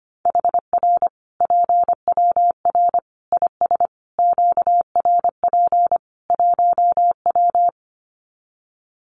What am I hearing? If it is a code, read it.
Morse code, heard as a keyed tone, about 25 wpm, 700 Hz. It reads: HR PWR IS QRP 1W
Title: Common QSO Phrases 25wpm